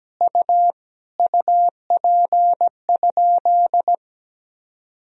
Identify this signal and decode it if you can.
Morse code, heard as a keyed tone, about 17 wpm, 700 Hz. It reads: U UP?
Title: Sentences from Top 500 Words and Abbreviations 17wpm